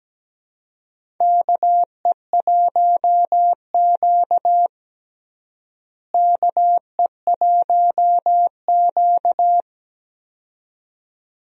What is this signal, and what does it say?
Morse code, heard as a keyed tone, about 17 wpm, 700 Hz. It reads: KE1Q KE1Q
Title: USA Call Signs - 4 Characters - Repeated 2x - 17wpm